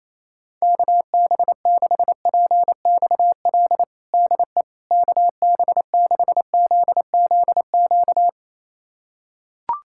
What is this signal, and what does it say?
Morse code, heard as a keyed tone, about 28 wpm, 700 Hz. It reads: KB6PXL DE KB6ZZQ
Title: USA Callsign DE Callsign - 6 Characters - 28wpm